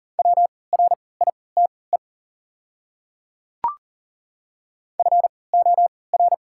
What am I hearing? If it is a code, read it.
Morse code, heard as a keyed tone, about 40 wpm, spaced as for 23 wpm, 690 Hz. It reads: WRITE FOR
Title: 3x Character Spacing - Top 300 Words - 40wpm